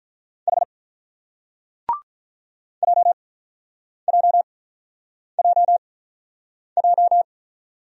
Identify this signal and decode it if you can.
Morse code, heard as a keyed tone, about 44 wpm, 700 Hz. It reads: H J J J J
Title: Speed-Racing Single Letters 35wpm